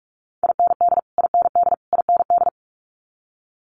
Morse code, heard as a keyed tone, about 46 wpm, 700 Hz. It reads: IND IND IND
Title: Multi-letter Phonemes Repeated 3x 45wpm